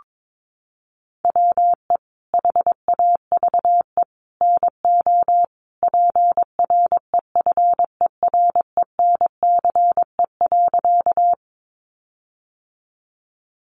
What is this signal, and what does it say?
Morse code, heard as a keyed tone, about 22 wpm, 700 Hz. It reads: WE HAVE NO PREFERENCE.
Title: Sentences from Top 1500 Words 22wpm